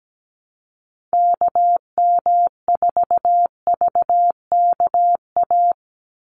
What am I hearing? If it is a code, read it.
Morse code, heard as a keyed tone, about 17 wpm, 700 Hz. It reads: KM4VKA
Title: USA Call Signs 17wpm